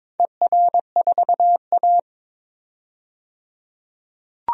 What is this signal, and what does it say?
Morse code, heard as a keyed tone, about 21 wpm, 700 Hz. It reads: ER4A
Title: DXCC-100 Call Signs - 4 Characters - 22wpm